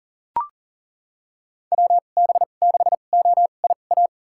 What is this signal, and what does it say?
Morse code, heard as a keyed tone, about 39 wpm, spaced as for 28 wpm, 700 Hz. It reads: WB6OIA
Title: USA Call Signs - 6 Characters - 2x Character Spacing - 40wpm